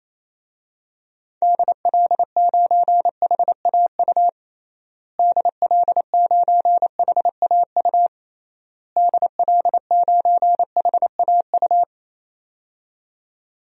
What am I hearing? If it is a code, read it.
Morse code, heard as a keyed tone, about 28 wpm, 700 Hz. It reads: DL9HAU DL9HAU DL9HAU
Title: DXCC-100 Call Signs - 6 Characters - Repeated 3x - 28wpm